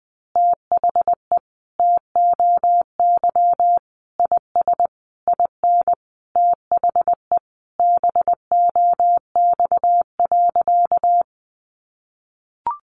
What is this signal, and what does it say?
Morse code, heard as a keyed tone, about 19 wpm, 700 Hz. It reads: THE TOY IS IN THE BOX.
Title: Easy Phrases 20wpm